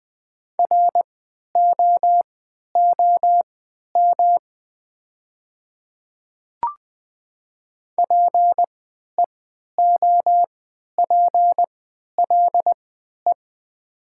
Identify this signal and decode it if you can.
Morse code, heard as a keyed tone, about 20 wpm, spaced as for 11 wpm, 700 Hz. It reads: ROOM PEOPLE
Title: 3x Character Spacing - Top 500 Words - 20wpm